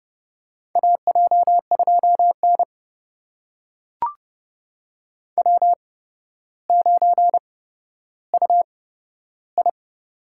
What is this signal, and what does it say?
Morse code, heard as a keyed tone, about 30 wpm, spaced as for 13 wpm, 700 Hz. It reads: AJ2N W9UI
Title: USA Call Signs - 4 Characters - 8x Character Spacing - 30wpm